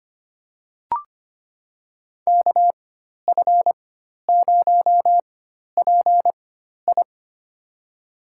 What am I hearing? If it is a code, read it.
Morse code, heard as a keyed tone, about 25 wpm, spaced as for 12 wpm, 700 Hz. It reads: KF0PI
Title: USA Call Signs - 5 Characters - 4x Character Spacing - 25wpm